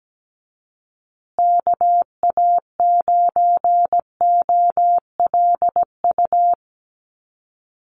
Morse code, heard as a keyed tone, about 17 wpm, 700 Hz. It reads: KA9OLU
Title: USA Call Signs Repeated 2x 17wpm